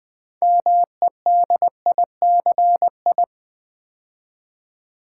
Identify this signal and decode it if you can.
Morse code, heard as a keyed tone, about 20 wpm, 700 Hz. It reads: MEDICI
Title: Build A Word - 3 Letters At A Time - 8-Letter Words - 20wpm